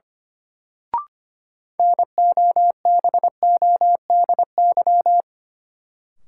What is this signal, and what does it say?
Morse code, heard as a keyed tone, about 25 wpm, 700 Hz. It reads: NOBODY
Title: Sentences from Top 1600 Words - New Word Drill Alt - 25wpm